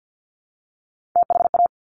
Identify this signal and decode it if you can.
Morse code, heard as a keyed tone, about 50 wpm, 700 Hz. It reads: THA
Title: Top Three Letter Combinations in Words 50wpm